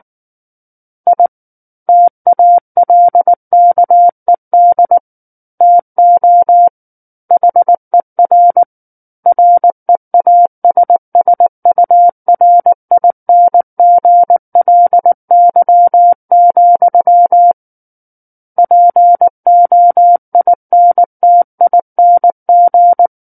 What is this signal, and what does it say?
Morse code, heard as a keyed tone, about 19 wpm, 700 Hz. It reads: I TALKED TO HER REASSURINGLY, POINTING
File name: Book One / 10-Chapter 10 - 19 WPM